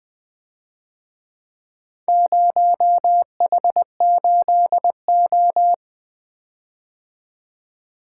Morse code, heard as a keyed tone, about 23 wpm, 700 Hz. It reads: 0H8O
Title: Speed-Racing - Four Letter-Number Combinations - 20wpm